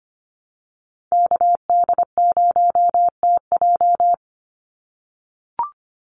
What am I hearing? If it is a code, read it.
Morse code, heard as a keyed tone, about 25 wpm, 700 Hz. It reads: KD0TJ
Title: USA Call Signs 25wpm